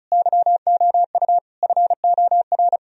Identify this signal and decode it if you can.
Morse code, heard as a keyed tone, about 46 wpm, 700 Hz. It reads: YOU FOR